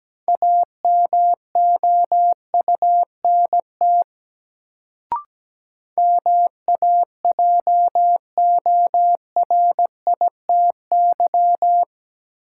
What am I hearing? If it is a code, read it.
Morse code, heard as a keyed tone, about 17 wpm, 700 Hz. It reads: AMOUNT MAJORITY
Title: Sets of 3 Words - New Word Drill - 17wpm